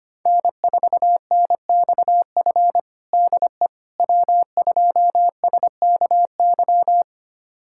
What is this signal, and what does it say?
Morse code, heard as a keyed tone, about 25 wpm, 700 Hz. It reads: N4NXF DE W2SKY
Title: USA Callsign DE Callsign - 5 Characters - 25wpm